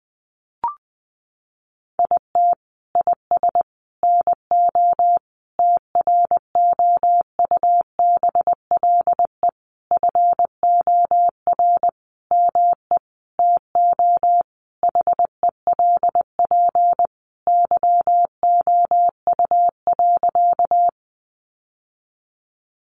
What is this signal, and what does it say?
Morse code, heard as a keyed tone, about 20 wpm, 700 Hz. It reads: IT IS NO TROUBLE FOR ME TO HELP YOU.
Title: Sentences from Top 1200 Words 20wpm